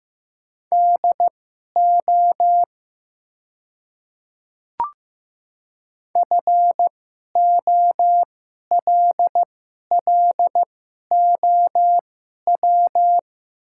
Morse code, heard as a keyed tone, about 15 wpm, spaced as for 11 wpm, 700 Hz. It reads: DO FOLLOW
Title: 2x Character Spacing - Top 300 Words - 15wpm